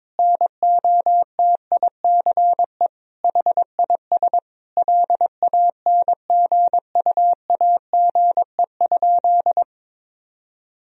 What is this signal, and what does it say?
Morse code, heard as a keyed tone, about 22 wpm, 700 Hz. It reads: NOTICE HIS LANGUAGE?
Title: Sentences from Top 1000 Words - Review Part 3 - 22wpm